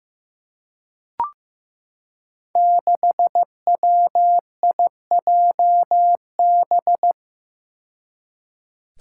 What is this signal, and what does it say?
Morse code, heard as a keyed tone, about 15 wpm, 700 Hz. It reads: 6WIJB
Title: Five Challenging Letter-Number Combinations 15wpm